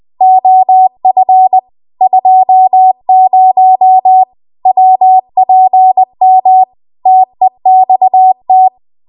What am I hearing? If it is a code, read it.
Morse code, heard as a keyed tone, about 20 wpm, 750 Hz. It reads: OF 20 WPM TEXT
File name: Archive / 20 WPM / 20WPM